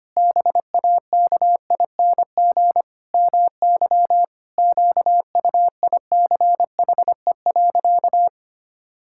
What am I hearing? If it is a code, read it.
Morse code, heard as a keyed tone, about 25 wpm, 700 Hz. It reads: BAKING MY QUICHE.
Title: Pangram Challenge v2 - 25wpm